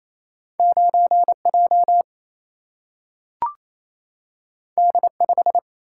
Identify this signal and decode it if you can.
Morse code, heard as a keyed tone, about 28 wpm, 700 Hz. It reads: 9J D5
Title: Single-Letter Single-Number 28wpm